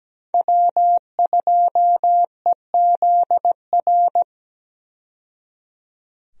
Morse code, heard as a keyed tone, about 17 wpm, 700 Hz. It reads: W2EZR
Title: USA Call Signs - 5 Characters - 17wpm